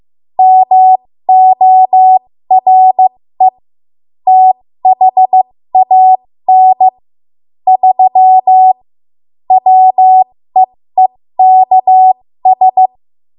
Morse code, heard as a keyed tone, about 15 wpm, spaced as for 13 wpm, 750 Hz. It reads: MORE THAN 3 WEEKS
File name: Archive / 13 WPM / 13WPM